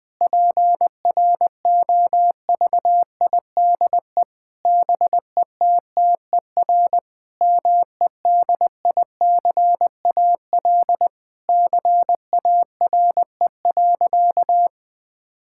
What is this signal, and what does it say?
Morse code, heard as a keyed tone, about 19 wpm, 700 Hz. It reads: PROVIDE BETTER MEDICAL CARE.
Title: Sentences from Top 1000 Words - Review Part 5 - 20wpm